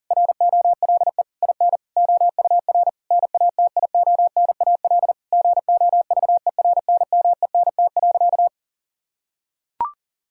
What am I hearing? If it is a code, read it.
Morse code, heard as a keyed tone, about 40 wpm, 700 Hz. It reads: ROLE IN OUR NATIONAL GOVERNMENT.
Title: Sentences from Top 1000 Words - Review Part 6 - 40wpm